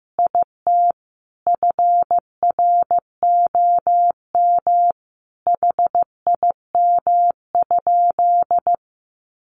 Morse code, heard as a keyed tone, about 14 wpm, 700 Hz. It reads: IT FROM HIM?
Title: Sentences from Top 1300 Words 15wpm